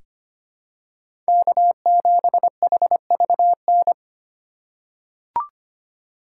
Morse code, heard as a keyed tone, about 25 wpm, 700 Hz. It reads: K7HVN
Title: USA Call Signs - 5 Characters - 4x Character Spacing - 25wpm